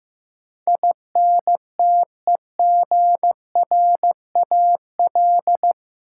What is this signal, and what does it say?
Morse code, heard as a keyed tone, about 15 wpm, 700 Hz. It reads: INTEGRAL
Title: Sets of 5 Words - New Word Drill - 15wpm